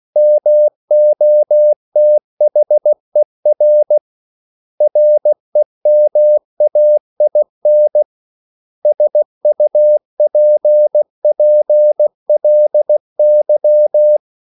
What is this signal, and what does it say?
Morse code, heard as a keyed tone, about 16 wpm, 600 Hz. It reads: MOTHER REMAIN SUPPLY
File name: CW / Practice / 500MostCommonEnglishWords16